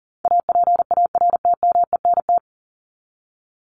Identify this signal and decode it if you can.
Morse code, heard as a keyed tone, about 40 wpm, 700 Hz. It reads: APARTMENT
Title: Top 751-1000 Words 40wpm